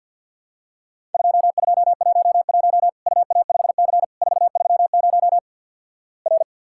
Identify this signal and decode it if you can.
Morse code, heard as a keyed tone, about 50 wpm, 700 Hz. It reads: 2211 UA5C 430 R
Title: Worked All Europe DX Contest - QTC Exchange with Repeats - 50wpm